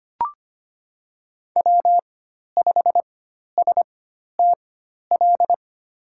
Morse code, heard as a keyed tone, about 25 wpm, spaced as for 12 wpm, 700 Hz. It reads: W5STL
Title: USA Call Signs - 5 Characters - 4x Character Spacing - 25wpm